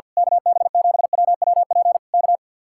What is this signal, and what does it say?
Morse code, heard as a keyed tone, about 48 wpm, 700 Hz. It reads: KB7WWP K
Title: Speed-Racing Common QSO Phrases - 50wpm